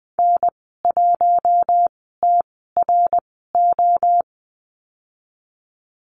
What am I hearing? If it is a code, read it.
Morse code, heard as a keyed tone, about 19 wpm, spaced as for 14 wpm, 700 Hz. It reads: N1TRO
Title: 082 Lesson 14g - Number 2 - Call Signs 20wpm